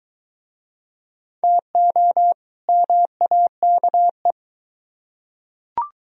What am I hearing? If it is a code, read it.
Morse code, heard as a keyed tone, about 21 wpm, 700 Hz. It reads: TO MAKE